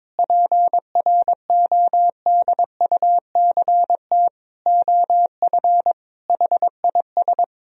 Morse code, heard as a keyed tone, about 22 wpm, 700 Hz. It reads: PRODUCT OF HIS
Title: Sentences from Top 1000 Words - Review Part 2 - 22wpm